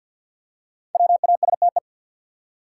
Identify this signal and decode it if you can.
Morse code, heard as a keyed tone, about 52 wpm, 700 Hz. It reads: WASTE
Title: Five Letter Words - 50wpm